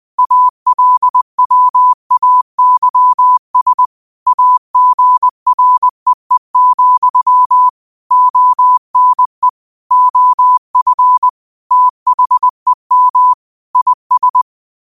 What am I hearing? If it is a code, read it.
Morse code, heard as a keyed tone, about 20 wpm, 1000 Hz. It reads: ALWAYS AGREE, ONE OF THEM IS
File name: qotdfiles / 20wpm